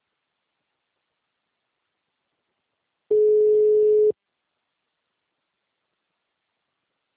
Ringback tone (Europe, 425 Hz).
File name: out- -2028- - - .29